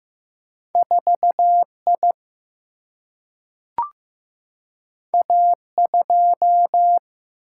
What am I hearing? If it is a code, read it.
Morse code, heard as a keyed tone, about 15 wpm, 700 Hz. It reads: 4 I A 2
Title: Single-Letter Single-Number 15wpm